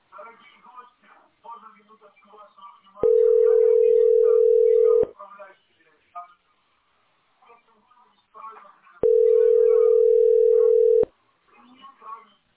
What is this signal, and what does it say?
Ringback tone (North America), 440 + 480 Hz, with speech.
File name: 58c2cb6a-e5a0-406e-8f77